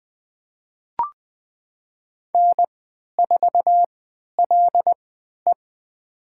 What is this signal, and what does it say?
Morse code, heard as a keyed tone, about 20 wpm, spaced as for 11 wpm, 700 Hz. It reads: N4LE
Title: USA Call Signs - 4 Characters - 3x Character Spacing - 20wpm